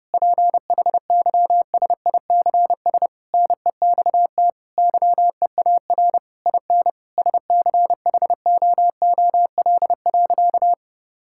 Morse code, heard as a keyed tone, about 30 wpm, 700 Hz. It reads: PHYSICS NEXT YEAR IN SCHOOL.
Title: Sentences from Top 1000 Words - Review Part 5 - 30wpm